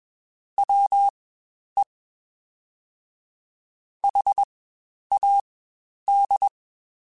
Morse code, heard as a keyed tone, about 21 wpm, spaced as for 10 wpm, 800 Hz. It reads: WE HAD
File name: Spaced Characters / Spaced Characters-mp3s / Weather 4 24 sp @20